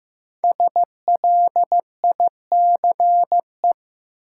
Morse code, heard as a keyed tone, about 15 wpm, 700 Hz. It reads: SLICE
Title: Five Letter Words - 15wpm